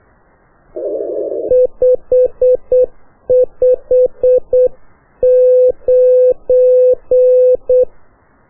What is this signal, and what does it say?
Morse code, heard as a keyed tone, about 8 wpm, 2600 Hz, with noise. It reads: 559
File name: qso1-qrs